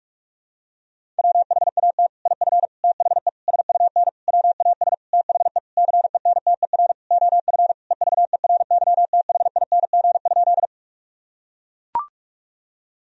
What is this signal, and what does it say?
Morse code, heard as a keyed tone, about 42 wpm, 700 Hz. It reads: WHAT IF THE SUN WAS THE CENTER OF EVERYTHING?
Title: Sentences from Top 1000 Words - Review Part 2 - 45wpm